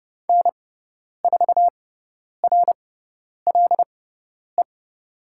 Morse code, heard as a keyed tone, about 30 wpm, spaced as for 10 wpm, 700 Hz. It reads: N4RLE
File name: Lesson 18g - Letter m - Call Signs 30wpm